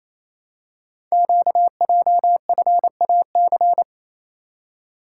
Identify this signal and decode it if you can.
Morse code, heard as a keyed tone, about 27 wpm, 700 Hz. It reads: QJFAC
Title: Code Groups - 5 Letters - 28wpm